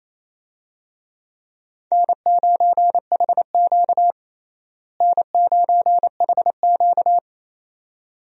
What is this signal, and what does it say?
Morse code, heard as a keyed tone, about 28 wpm, 700 Hz. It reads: N9HQ N9HQ
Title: USA Call Signs - 4 Characters - Repeated 2x - 28wpm